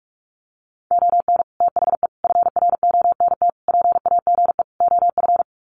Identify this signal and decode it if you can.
Morse code, heard as a keyed tone, about 44 wpm, 700 Hz. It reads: ON THE FRONT PAGE OF
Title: Sets of 5 Words Encore 45wpm